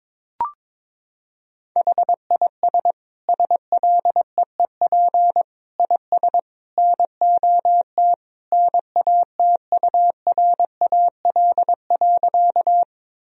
Morse code, heard as a keyed tone, about 22 wpm, 700 Hz. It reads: HIS SLEEP IS NOT NATURAL.
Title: Sentences from Top 1000 Words - Review Part 4 - 22wpm